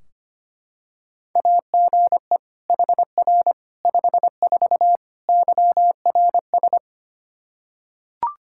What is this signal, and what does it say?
Morse code, heard as a keyed tone, about 25 wpm, 700 Hz. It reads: AGE HR 54 YRS
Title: Practice QSO - Exchange 3 - More Info - 25wpm